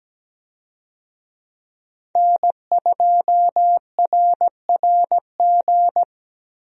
Morse code, heard as a keyed tone, about 17 wpm, 700 Hz. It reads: N2RRG
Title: USA Call Signs - 5 Characters - Repeated 3x - 17wpm